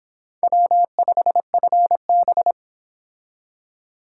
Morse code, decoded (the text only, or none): W5FB